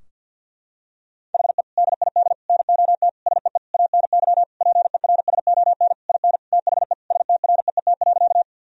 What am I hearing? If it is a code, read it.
Morse code, heard as a keyed tone, about 48 wpm, 700 Hz. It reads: HE DID NOT SEE ANY PERSON IN THE STREET.
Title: Sentences from Top 1000 Words - Review Part 3 - 50wpm